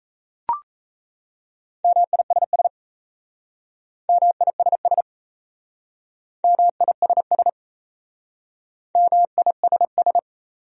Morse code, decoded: MISS MISS MISS MISS

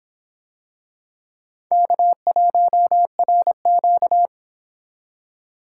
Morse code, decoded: K1RQ

